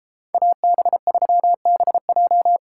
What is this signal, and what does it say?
Morse code, heard as a keyed tone, about 34 wpm, 700 Hz. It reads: AB3BJ